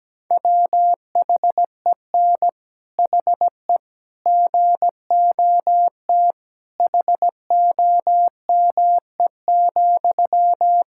Morse code, heard as a keyed tone, about 17 wpm, 700 Hz. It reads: WHEN HE GOT HOME,
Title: Sentences from Top 1000 Words - Review Part 6 - 17wpm